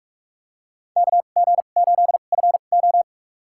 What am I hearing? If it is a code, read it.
Morse code, heard as a keyed tone, about 46 wpm, spaced as for 33 wpm, 700 Hz. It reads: KG8FO
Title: USA Call Signs - 5 Characters - 2x Character Spacing - 45wpm